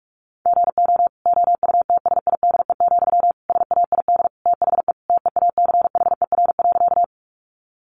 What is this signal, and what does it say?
Morse code, heard as a keyed tone, about 44 wpm, 700 Hz. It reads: GO OUTSIDE, SAID THE TEACHER.
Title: Sentences from Top 900 Words 45wpm